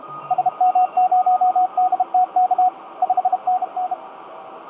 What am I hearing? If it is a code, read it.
Morse code, heard as a keyed tone, about 32 wpm, 750 Hz, with radio noise. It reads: SM0DTK 5NN
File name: tt8tt-10m-cw-here-in-jn90ab